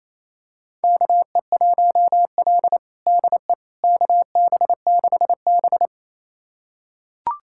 Morse code, heard as a keyed tone, about 28 wpm, 700 Hz. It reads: KE1L DE KB6B
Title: USA Callsign DE Callsign - 4 Characters - 28wpm